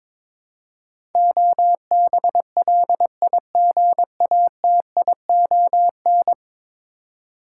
Morse code, heard as a keyed tone, about 22 wpm, 700 Hz. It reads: OBLIGATION
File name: practice / 22wpm / 10 Letter Words - First 3 Letters Given - 22wpm